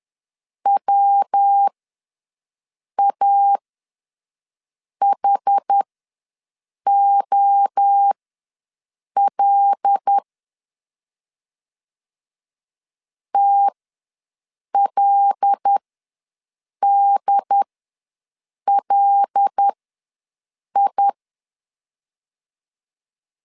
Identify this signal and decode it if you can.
Morse code, heard as a keyed tone, about 10 wpm, spaced as for 6 wpm, 800 Hz. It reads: WAHOL TLDLI
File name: LEZIONI / 5 LBVC / 5 L ws 10wpm 3636